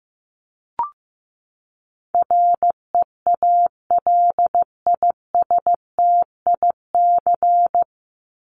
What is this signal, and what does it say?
Morse code, heard as a keyed tone, about 15 wpm, 700 Hz. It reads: REALISTIC
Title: Top 1751-2000 Words 15wpm